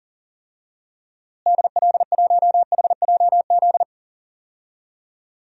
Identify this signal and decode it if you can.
Morse code, heard as a keyed tone, about 40 wpm, 700 Hz. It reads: DL1HJZ
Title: DXCC-100 Call Signs - 6 Characters - Repeated 3x - 40wpm